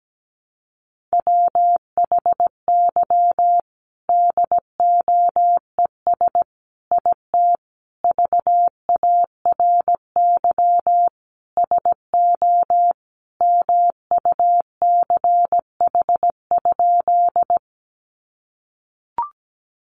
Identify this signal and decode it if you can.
Morse code, heard as a keyed tone, about 17 wpm, 700 Hz. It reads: WHY DOES IT VARY SO MUCH?
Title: Sentences from Top 1200 Words 17wpm